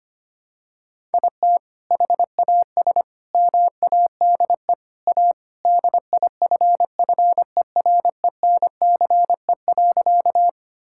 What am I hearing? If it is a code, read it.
Morse code, heard as a keyed tone, about 25 wpm, 700 Hz. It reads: IT HAS MADE A DIFFERENCE.
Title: Sentences from Top 1000 Words - Review Part 5 - 25wpm